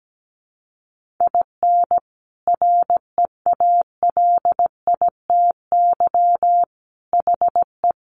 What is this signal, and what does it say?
Morse code, heard as a keyed tone, about 17 wpm, 700 Hz. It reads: IN REALITY HE
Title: Sentences from Top 1200 Words 17wpm